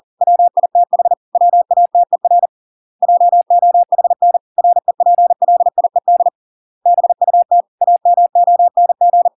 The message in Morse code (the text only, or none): WITH WATER JOHN REPLIED BUT AMONG